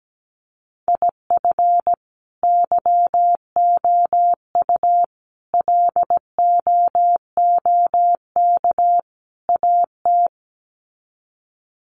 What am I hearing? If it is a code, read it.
Morse code, heard as a keyed tone, about 17 wpm, 700 Hz. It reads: IF YOU LOOK AT
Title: Sets of 4 Words 17wpm